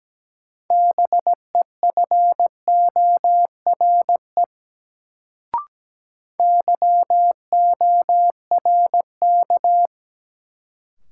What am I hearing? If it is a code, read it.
Morse code, heard as a keyed tone, about 17 wpm, 700 Hz. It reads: BEFORE YORK
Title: Sets of 2 Words - New Word Drill - 17wpm